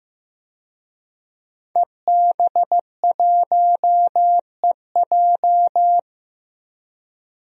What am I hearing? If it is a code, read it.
Morse code, heard as a keyed tone, about 15 wpm, 700 Hz. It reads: EB1EJ
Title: DXCC-100 Call Signs - 5 Characters - 15wpm